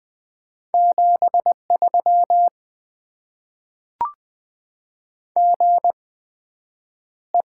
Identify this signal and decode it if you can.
Morse code, heard as a keyed tone, about 20 wpm, spaced as for 9 wpm, 700 Hz. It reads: 73 GE
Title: Top 20 QSO Elements - 8x Character Spacing - 20wpm